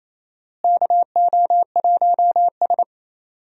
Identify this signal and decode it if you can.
Morse code, heard as a keyed tone, about 28 wpm, 700 Hz. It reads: KO1S
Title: USA Call Signs - 4 Characters - Repeated 3x - 28wpm